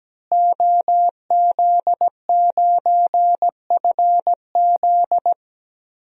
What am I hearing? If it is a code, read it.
Morse code, heard as a keyed tone, about 17 wpm, 700 Hz. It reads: OZ9FZ
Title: DXCC-100 Call Signs - 5 Characters - Repeated 3x - 17wpm